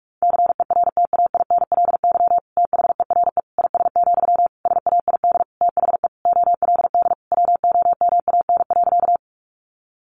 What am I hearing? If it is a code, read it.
Morse code, heard as a keyed tone, about 44 wpm, 700 Hz. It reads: CERTAINLY THERE IS, SAID THE OLD WOMAN.
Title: Sentences from Top 1100 Words 45wpm